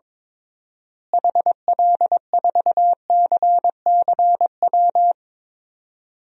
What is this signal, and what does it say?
Morse code, heard as a keyed tone, about 22 wpm, 700 Hz. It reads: HL4CCW
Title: DXCC-100 Call Signs - 6 Characters - Repeated 3x - 22wpm